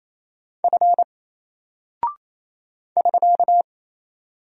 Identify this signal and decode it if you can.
Morse code, heard as a keyed tone, about 27 wpm, 700 Hz. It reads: F <SK>
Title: Single Character (Letter:Number:Punctuation:Prosign) 28wpm